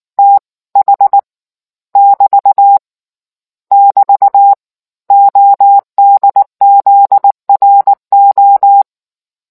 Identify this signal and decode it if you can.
Morse code, heard as a keyed tone, about 19 wpm, 800 Hz. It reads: TH <BT> <BT> ODZRO